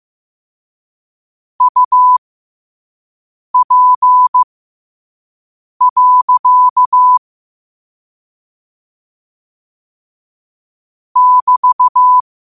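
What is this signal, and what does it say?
Morse code, heard as a keyed tone, about 15 wpm, spaced as for 5 wpm, 1000 Hz. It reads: UP. <BT>
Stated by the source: Morse code sent at 5 wpm.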